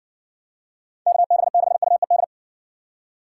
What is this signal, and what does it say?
Morse code, heard as a keyed tone, about 59 wpm, 700 Hz. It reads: KB6UED